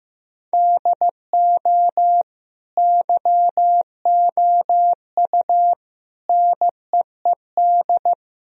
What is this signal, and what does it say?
Morse code, heard as a keyed tone, about 16 wpm, 700 Hz. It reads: DO YOU NEED
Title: Sentences from Top 1000 Words - Review Part 3 - 15wpm